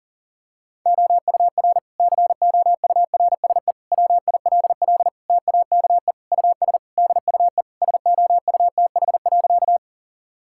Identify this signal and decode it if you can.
Morse code, heard as a keyed tone, about 40 wpm, 700 Hz. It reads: OUR COURSE WILL TAKE US DUE SOUTH.
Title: Sentences from Top 1000 Words - Review Part 6 - 40wpm